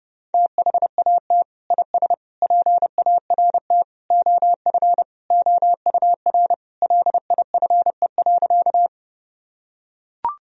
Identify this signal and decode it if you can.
Morse code, heard as a keyed tone, about 32 wpm, 700 Hz. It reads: THAT IS PART OF OUR LIFE.
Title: Sentences from Top 1000 Words - Review Part 1 - 30wpm